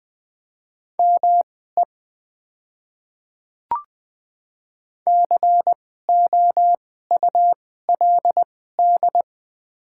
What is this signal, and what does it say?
Morse code, heard as a keyed tone, about 20 wpm, spaced as for 15 wpm, 700 Hz. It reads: ME COULD
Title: 2x Character Spacing - Top 200 Words - 20wpm